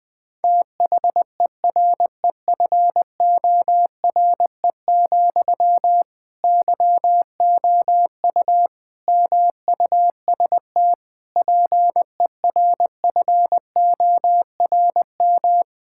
Morse code, heard as a keyed tone, about 20 wpm, 700 Hz. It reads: THEREFORE, YOU MUST PERFORM
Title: Sentences from Top 1000 Words - Review Part 5 - 20wpm